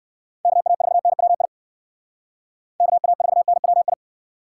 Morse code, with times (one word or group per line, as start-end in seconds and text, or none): 0.5-1.5 KA4NWI
2.8-3.9 KA4NWI